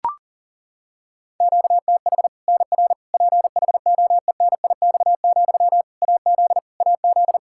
Morse code, heard as a keyed tone, about 37 wpm, 700 Hz. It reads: QTH NR PHOENIX, AZ AZ
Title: K1USN SST QSO Practice 40wpm